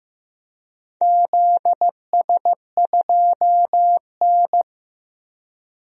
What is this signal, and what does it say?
Morse code, heard as a keyed tone, about 15 wpm, 700 Hz. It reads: ZS2N
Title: DXCC-100 Call Signs - 4 Characters - 15wpm